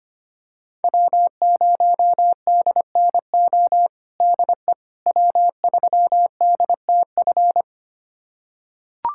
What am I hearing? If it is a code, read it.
Morse code, heard as a keyed tone, about 25 wpm, 700 Hz. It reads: W0DNO DE W3DTF
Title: USA Callsign DE Callsign - 5 Characters - 25wpm